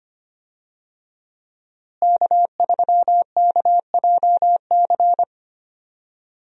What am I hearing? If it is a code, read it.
Morse code, heard as a keyed tone, about 25 wpm, 700 Hz. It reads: K3KJC
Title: USA Call Signs - 5 Characters - Repeated 2x - 25wpm